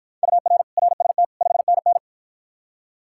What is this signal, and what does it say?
Morse code, heard as a keyed tone, about 53 wpm, 700 Hz. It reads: UR RST 5NN